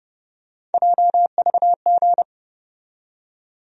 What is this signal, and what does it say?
Morse code, heard as a keyed tone, about 30 wpm, 700 Hz. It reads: JVG
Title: USA Call Sign Postfixes - 3 Letters 30wpm